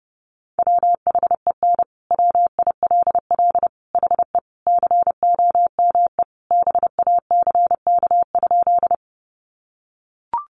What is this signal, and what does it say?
Morse code, heard as a keyed tone, about 28 wpm, 700 Hz. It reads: WHEN WILL HE COME BACK?
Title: Sentences from Top 200 Words 30wpm